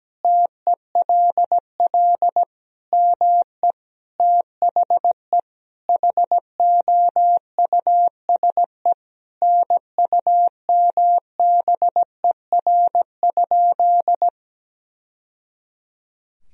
Morse code, heard as a keyed tone, about 17 wpm, 700 Hz. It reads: TELL ME THE HOUSE NUMBER?
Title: Sentences from Top 1000 Words - Review Part 1 - 17wpm